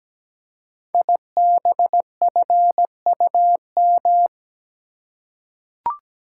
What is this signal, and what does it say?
Morse code, heard as a keyed tone, about 17 wpm, 690 Hz. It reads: IBFUM
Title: Code Groups - 5 Letter:Numbers - 17wpm